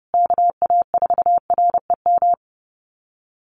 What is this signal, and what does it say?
Morse code, heard as a keyed tone, about 30 wpm, 700 Hz. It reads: KA4REM
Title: USA Call Signs Repeated 2x 30wpm